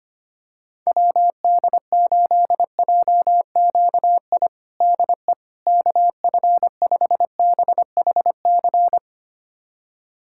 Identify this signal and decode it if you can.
Morse code, heard as a keyed tone, about 25 wpm, 700 Hz. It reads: WD8JQI DE KF5BHC
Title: USA Callsign DE Callsign 25wpm